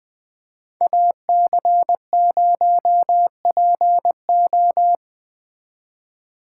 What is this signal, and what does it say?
Morse code, heard as a keyed tone, about 20 wpm, 700 Hz. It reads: AC0PO
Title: Speed Racing - USA 2x2 Call Signs 17wpm